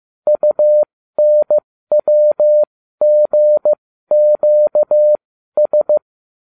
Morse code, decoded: UNWGQS

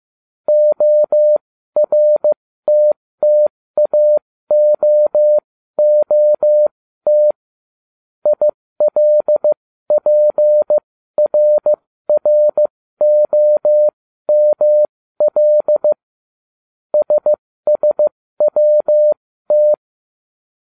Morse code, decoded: ORTTAOOT ILPRROML SSWT

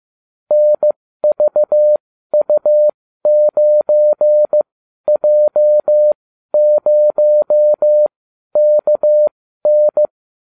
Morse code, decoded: NVU9J0KN

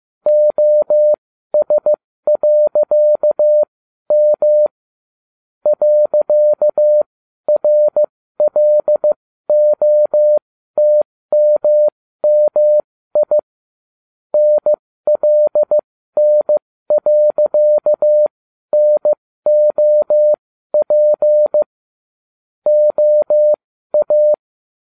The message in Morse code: OS.M .RLOTMMI NLN.NOP OA

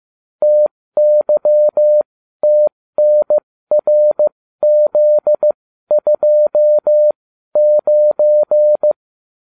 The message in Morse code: TYTNRZ29

